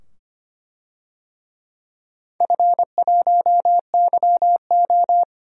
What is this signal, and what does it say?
Morse code, heard as a keyed tone, about 25 wpm, 700 Hz. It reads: F1YO
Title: DXCC-100 Call Signs - 4 Characters - 25wpm